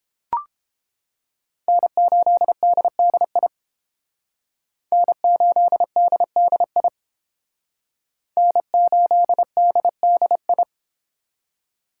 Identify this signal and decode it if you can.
Morse code, heard as a keyed tone, about 28 wpm, 700 Hz. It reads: N8DDI N8DDI N8DDI